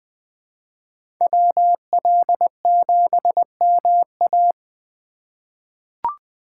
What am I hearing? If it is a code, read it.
Morse code, heard as a keyed tone, about 21 wpm, 700 Hz. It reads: WL7MA
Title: USA Call Signs - 3x Character Spacing - 20wpm